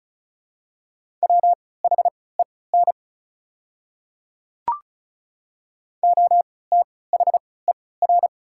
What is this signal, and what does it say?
Morse code, heard as a keyed tone, about 34 wpm, spaced as for 20 wpm, 700 Hz. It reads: WHEN OTHER
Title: 3x Character Spacing - Top 300 Words - 35wpm